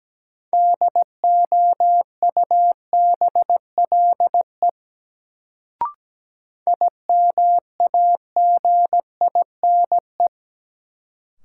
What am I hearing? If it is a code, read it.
Morse code, heard as a keyed tone, about 17 wpm, 700 Hz. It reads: DOUBLE IMAGINE
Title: Sentences from Top 1400 Words - New Word Drill - 17wpm